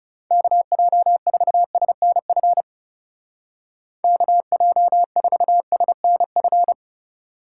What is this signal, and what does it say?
Morse code, heard as a keyed tone, about 36 wpm, 700 Hz. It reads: KJ4SNF KJ4SNF
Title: Speed Racing - USA 2x3 Call Signs 30wpm